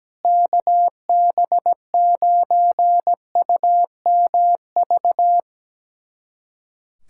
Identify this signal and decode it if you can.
Morse code, heard as a keyed tone, about 17 wpm, 700 Hz. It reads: KB9UMV
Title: USA Callsign DE Callsign 17wpm